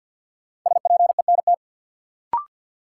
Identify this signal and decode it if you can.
Morse code, heard as a keyed tone, about 52 wpm, 700 Hz. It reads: SPENT
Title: Sets of 5 Words - New Word Drill - 50wpm